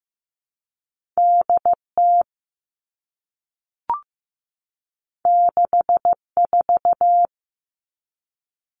Morse code, heard as a keyed tone, about 15 wpm, 700 Hz. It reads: DT 64
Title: All Cut Numbers - 2-Digit 15wpm